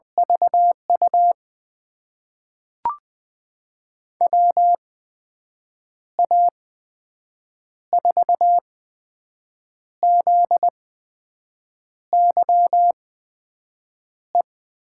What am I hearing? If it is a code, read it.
Morse code, heard as a keyed tone, about 20 wpm, spaced as for 6 wpm, 700 Hz. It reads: VU WA4ZYE
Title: USA Call Signs - 6 Characters - 8x Character Spacing - 20wpm